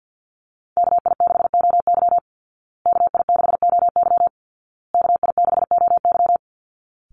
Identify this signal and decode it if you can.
Morse code, heard as a keyed tone, about 50 wpm, 700 Hz. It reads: KI6OY KI6OY KI6OY
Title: Call Signs Repeated 2x - 50wpm